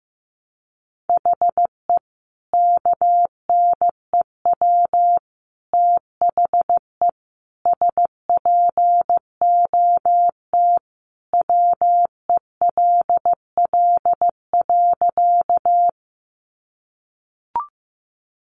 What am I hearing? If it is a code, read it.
Morse code, heard as a keyed tone, about 14 wpm, 700 Hz. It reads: HE KNEW THE SPOT WELL.
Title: Sentences from Top 1100 Words 15wpm